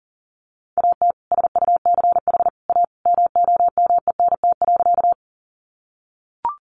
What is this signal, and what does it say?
Morse code, heard as a keyed tone, about 40 wpm, 700 Hz. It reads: AT SUCH A MOMENT.
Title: Sentences from Top 1000 Words 40wpm